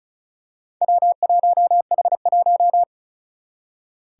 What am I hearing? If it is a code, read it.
Morse code, heard as a keyed tone, about 36 wpm, 700 Hz. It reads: W1H1
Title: Four Challenging Letter-Number Combinations 35wpm